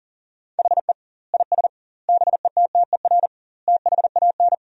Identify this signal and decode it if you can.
Morse code, heard as a keyed tone, about 40 wpm, 700 Hz. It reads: HE IS BETTER THAN
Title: 2x Word-Spacing - Sentences from Top 400 Words - 40wpm